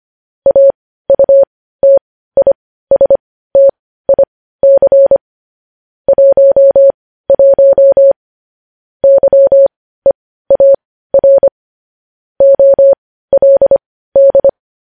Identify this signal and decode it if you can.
Morse code, heard as a keyed tone, about 25 wpm, spaced as for 15 wpm, 550 Hz. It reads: AUTISTIC 11 YEAR OLD
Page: news-headlines-in-morse-code-at-15-wpm